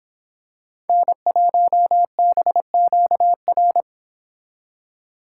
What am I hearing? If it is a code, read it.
Morse code, heard as a keyed tone, about 26 wpm, 700 Hz. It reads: N1BQR